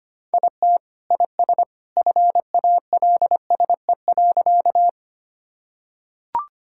Morse code, heard as a keyed tone, about 25 wpm, 700 Hz. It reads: IT IS FALSE.
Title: Sentences from Top 1600 Words 25wpm